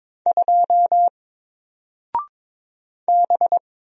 Morse code, heard as a keyed tone, about 22 wpm, 700 Hz. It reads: 2 B
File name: Single-Letter-Number-22wpm